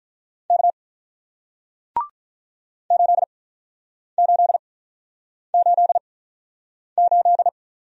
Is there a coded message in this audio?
Yes: Morse code from 0.5 to 7.5 s, heard as a keyed tone, about 47 wpm, spaced as for 9 wpm, 700 Hz. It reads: K 8888